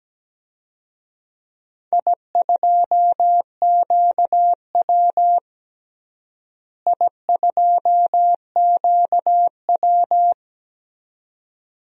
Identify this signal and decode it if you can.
Morse code, heard as a keyed tone, about 17 wpm, 700 Hz. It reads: I2QW I2QW
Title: DXCC-100 Call Signs - 4 Characters - Repeated 2x - 17wpm